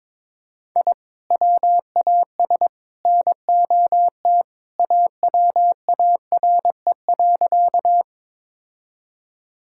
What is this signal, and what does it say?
Morse code, heard as a keyed tone, about 22 wpm, 700 Hz. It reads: I WAS NOT AWARE.
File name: practice / 22wpm / Sentences with 3-Word Phrases - 22wpm